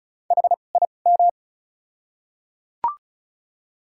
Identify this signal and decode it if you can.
Morse code, heard as a keyed tone, about 34 wpm, spaced as for 25 wpm, 700 Hz. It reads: HIM
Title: 2x Character Spacing - Top 300 Words - 35wpm